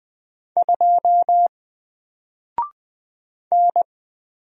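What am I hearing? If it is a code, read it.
Morse code, heard as a keyed tone, about 20 wpm, 700 Hz. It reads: 2 N
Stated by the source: Single letter and number Morse drill at 20 wpm, from N4LCD.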